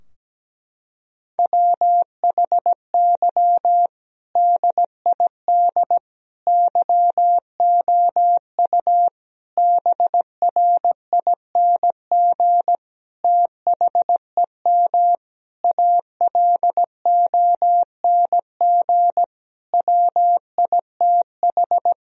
Morse code, heard as a keyed tone, about 17 wpm, 700 Hz. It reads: WHY DID YOU BRING THEM ALONG WITH
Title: Sentences from Top 1000 Words - Review Part 3 - 17wpm